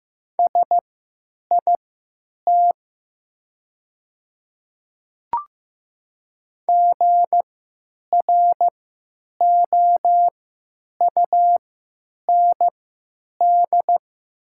Morse code, heard as a keyed tone, about 15 wpm, spaced as for 9 wpm, 700 Hz. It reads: SIT GROUND
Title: 3x Character Spacing - Top 500 Words - 15wpm